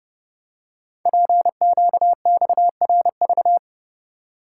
Morse code, heard as a keyed tone, about 30 wpm, 700 Hz. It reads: PQXRV
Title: Code Groups - 5 Letters - 30wpm